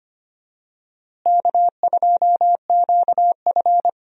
Morse code, heard as a keyed tone, about 25 wpm, 700 Hz. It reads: K2QF
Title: USA Call Signs - 4 Characters - Repeated 2x - 25wpm